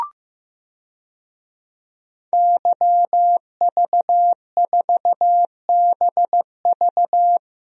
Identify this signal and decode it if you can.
Morse code, heard as a keyed tone, about 15 wpm, 700 Hz. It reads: YV4BV
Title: DXCC-100 Call Signs - 5 Characters - Repeated 3x - 15wpm